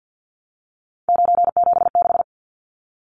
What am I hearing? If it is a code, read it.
Morse code, heard as a keyed tone, about 50 wpm, 700 Hz. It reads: 976
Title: Numbers - 3-Digit 50wpm